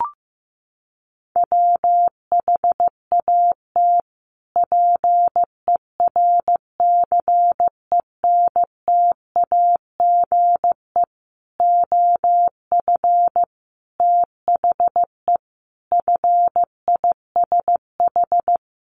Morse code, heard as a keyed tone, about 15 wpm, 700 Hz. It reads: WHAT PERCENTAGE OF THE FISH
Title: Sentences from Top 1300 Words 15wpm